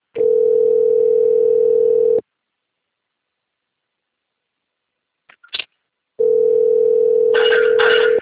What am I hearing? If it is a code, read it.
Ringback tone (North America), 440 + 480 Hz.